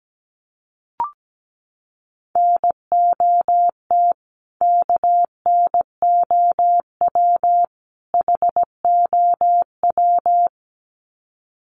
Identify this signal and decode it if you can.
Morse code, heard as a keyed tone, about 17 wpm, 700 Hz. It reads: NOT KNOW HOW
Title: Sets of 3 Words 17wpm